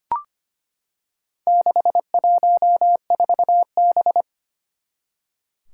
Morse code, heard as a keyed tone, about 25 wpm, 700 Hz. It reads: 614B
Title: Four Challenging Letter-Number Combinations 25wpm